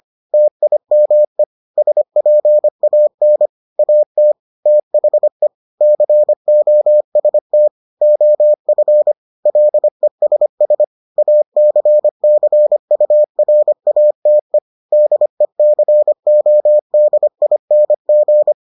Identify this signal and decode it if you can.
Morse code, heard as a keyed tone, about 25 wpm, 600 Hz. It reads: TIME SPAN AT THE COST OF LESS ACCURATE DECODING